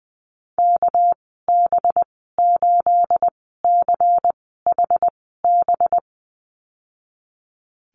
Morse code, heard as a keyed tone, about 20 wpm, spaced as for 14 wpm, 700 Hz. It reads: KB8CHB
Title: 213 Lesson 34g - Number 8 - Call Signs 20wpm